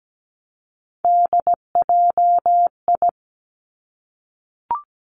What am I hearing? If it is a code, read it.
Morse code, heard as a keyed tone, about 17 wpm, 700 Hz. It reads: DJI
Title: Three Challenging Letter-Number Combinations 17wpm